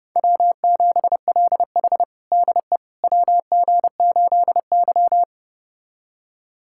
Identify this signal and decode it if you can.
Morse code, heard as a keyed tone, about 30 wpm, 700 Hz. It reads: W7LH DE WG8Y
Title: USA Callsign DE Callsign - 4 Characters - 30wpm